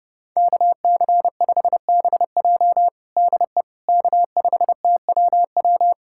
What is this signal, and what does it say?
Morse code, heard as a keyed tone, about 30 wpm, 700 Hz. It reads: KC5BJ DE K5TWW
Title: USA Callsign DE Callsign - 5 Characters - 30wpm